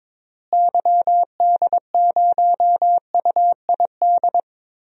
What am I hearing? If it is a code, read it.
Morse code, heard as a keyed tone, about 22 wpm, 700 Hz. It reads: YD0UID
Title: DXCC-100 Call Signs - 6 Characters - Repeated 3x - 22wpm